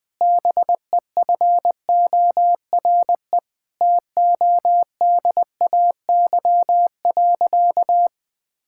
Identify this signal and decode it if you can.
Morse code, heard as a keyed tone, about 20 wpm, 700 Hz. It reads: BEFORE TODAY.
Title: Sentences from Top 1000 Words - Review Part 6 - 20wpm